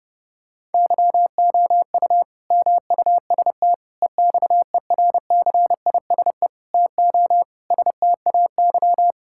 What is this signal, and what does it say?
Morse code, heard as a keyed tone, about 30 wpm, 700 Hz. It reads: YOU MUST EXERCISE TO STAY
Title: Sentences from Top 1000 Words - Review Part 6 - 30wpm